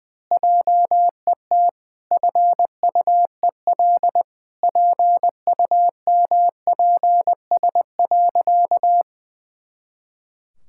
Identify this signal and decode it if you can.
Morse code, heard as a keyed tone, about 20 wpm, 700 Hz. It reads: JET FUEL PUMPS.
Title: Pangram Challenge v2 - 20wpm